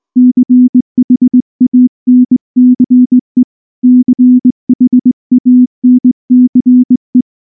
Morse code, heard as a keyed tone, about 19 wpm, 250 Hz. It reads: CHANCE CHANCE
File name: morse-training-20WPM